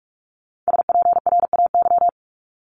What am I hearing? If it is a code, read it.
Morse code, heard as a keyed tone, about 46 wpm, 690 Hz. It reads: SPRAY
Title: Multi-letter Phonemes Repeated 3x 45wpm